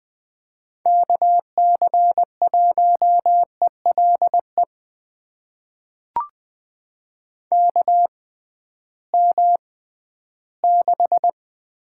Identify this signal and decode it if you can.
Morse code, heard as a keyed tone, about 20 wpm, spaced as for 13 wpm, 700 Hz. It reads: KC1ELE KM6
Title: USA Call Signs - 6 Characters - 6x Character Spacing - 20wpm